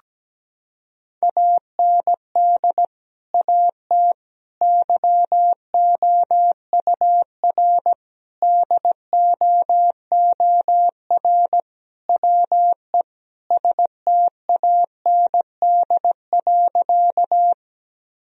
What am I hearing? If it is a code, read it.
Morse code, heard as a keyed tone, about 17 wpm, 700 Hz. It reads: AND AT YOUR DOOR WE STAND.
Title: Sentences from Top 1000 Words - Review Part 1 - 17wpm